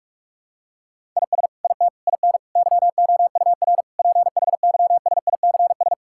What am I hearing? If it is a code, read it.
Morse code, heard as a keyed tone, about 41 wpm, 700 Hz. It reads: IS IT IN YOUR PHYSICS
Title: Sentences from Top 1000 Words - Review Part 5 - 45wpm